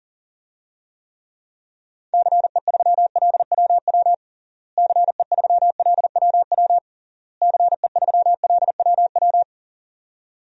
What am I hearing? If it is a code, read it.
Morse code, heard as a keyed tone, about 40 wpm, 700 Hz. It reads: CE3LWW CE3LWW CE3LWW
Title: DXCC-100 Call Signs - 6 Characters - Repeated 3x - 40wpm